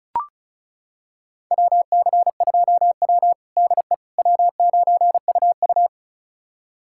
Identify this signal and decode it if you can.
Morse code, heard as a keyed tone, about 36 wpm, 690 Hz. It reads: WC2W DE W9UU
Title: USA Callsign DE Callsign - 4 Characters - 35wpm